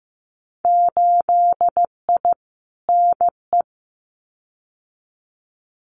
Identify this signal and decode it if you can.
Morse code, heard as a keyed tone, about 15 wpm, 700 Hz. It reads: 8I NE
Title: Winter Field Day Exchanges Encore 15wpm